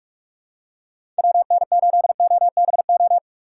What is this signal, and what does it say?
Morse code, heard as a keyed tone, about 42 wpm, 700 Hz. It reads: WN8OBO
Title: USA Call Signs - 6 Characters - 6x Character Spacing - 45wpm